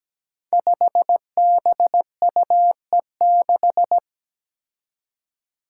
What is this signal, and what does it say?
Morse code, heard as a keyed tone, about 17 wpm, 700 Hz. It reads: 5BUE6
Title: Five Challenging Letter-Number Combinations 17wpm